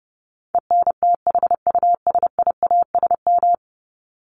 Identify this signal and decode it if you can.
Morse code, heard as a keyed tone, about 30 wpm, 700 Hz. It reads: ENTHUSIASM
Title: Top 1751-2000 Words 30wpm